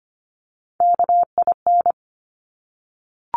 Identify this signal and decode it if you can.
Morse code, heard as a keyed tone, about 25 wpm, 700 Hz. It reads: KIN
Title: Top Three Letter Combinations in Words 25wpm